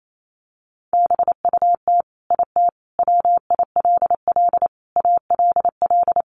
Morse code, heard as a keyed tone, about 28 wpm, 700 Hz. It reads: BUT IT WILL ALL
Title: Sentences from Top 1400 Words 28wpm